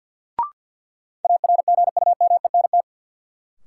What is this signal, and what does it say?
Morse code, heard as a keyed tone, about 50 wpm, 700 Hz. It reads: ARGUMENT
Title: Sentences from Top 1400 Words - New Word Drill - 50wpm